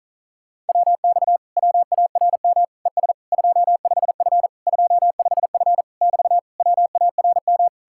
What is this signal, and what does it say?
Morse code, heard as a keyed tone, about 48 wpm, 700 Hz. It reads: WX WARM ES 25F 25F <BT> WARM